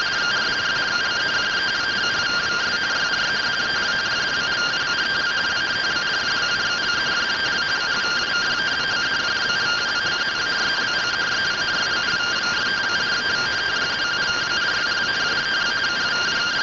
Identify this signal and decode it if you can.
Two-tone teleprinter signal, about 170 Hz shift.